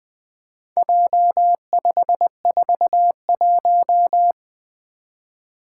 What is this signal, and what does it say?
Morse code, heard as a keyed tone, about 20 wpm, 700 Hz. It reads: J541
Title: Four Letter-Number Combinations - 20wpm